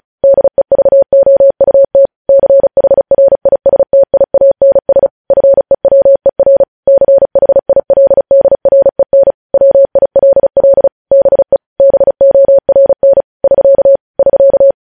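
Morse code, heard as a keyed tone, about 36 wpm, 550 Hz. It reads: DEVOUT CHRISTIANS FEWER CHILDREN WILL BE BORN <SK> <SK>
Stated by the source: Morse practice sending news headlines at 35 wpm.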